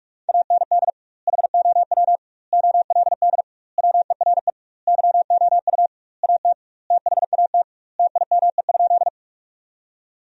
Morse code, heard as a keyed tone, about 41 wpm, 700 Hz. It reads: AND HOW OLD WERE YOU AT THAT TIME?
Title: 2x Word-Spacing - Sentences from Top 200 Words - 45wpm